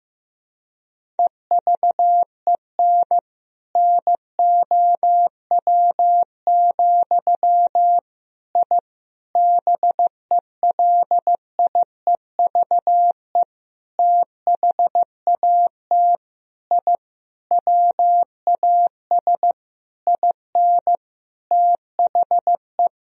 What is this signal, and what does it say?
Morse code, heard as a keyed tone, about 15 wpm, 700 Hz. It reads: EVEN NOW, I BELIEVE THAT I WAS IN THE
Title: Sentences from Top 1000 Words - Review Part 1 - 15wpm